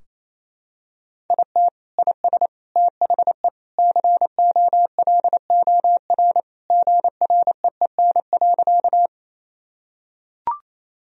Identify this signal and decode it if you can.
Morse code, heard as a keyed tone, about 28 wpm, 700 Hz. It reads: IT IS THE COLOR GREEN.
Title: Sentences from Top 1000 Words - Review Part 2 - 28wpm